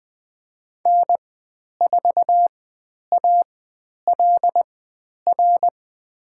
Morse code, heard as a keyed tone, about 20 wpm, spaced as for 10 wpm, 700 Hz. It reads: N4ALR
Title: 102 Lesson 17g - Letter u - Call Signs 20wpm